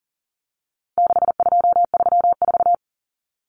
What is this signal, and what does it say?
Morse code, heard as a keyed tone, about 40 wpm, 700 Hz. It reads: 6234
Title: Common Cut Numbers - 4-Digit 40wpm